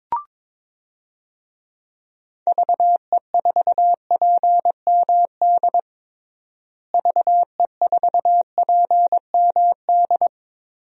Morse code, heard as a keyed tone, about 22 wpm, 700 Hz. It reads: VE4PMD VE4PMD
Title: DXCC-100 Call Signs - 6 Characters - Repeated 3x - 22wpm